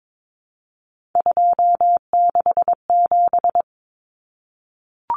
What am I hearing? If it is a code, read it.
Morse code, heard as a keyed tone, about 22 wpm, 700 Hz. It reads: 267
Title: Common Cut Numbers - 3-Digit 22wpm